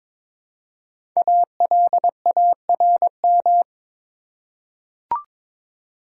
Morse code, heard as a keyed tone, about 22 wpm, 700 Hz. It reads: ALARM
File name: 5 Letter Words - First 3 Letters Given - 22wpm